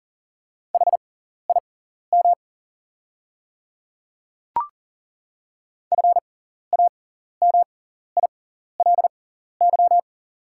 Morse code, heard as a keyed tone, about 40 wpm, spaced as for 14 wpm, 690 Hz. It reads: HIM FAMILY
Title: 6x Character Spacing - Top 500 Words - 40wpm